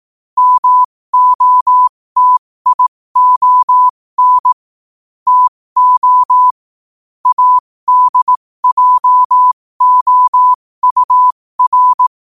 Morse code, heard as a keyed tone, about 18 wpm, spaced as for 15 wpm, 1000 Hz. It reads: MOTION TO ADJOUR
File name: qotdfiles / 15wpm